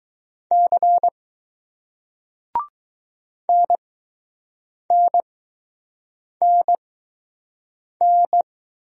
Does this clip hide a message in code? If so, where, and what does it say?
Yes: Morse code from 0.5 to 8.4 s, heard as a keyed tone, about 21 wpm, spaced as for 6 wpm, 700 Hz. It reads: C NNNN